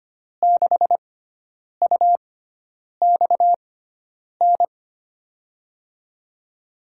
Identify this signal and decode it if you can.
Morse code, heard as a keyed tone, about 25 wpm, spaced as for 9 wpm, 700 Hz. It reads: 6UXN
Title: USA Call Signs - 6 Characters - 6x Character Spacing - 25wpm